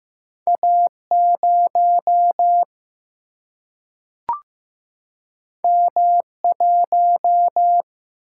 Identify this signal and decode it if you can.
Morse code, heard as a keyed tone, about 15 wpm, 700 Hz. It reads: A0 M1
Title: Single-Letter Single-Number 15wpm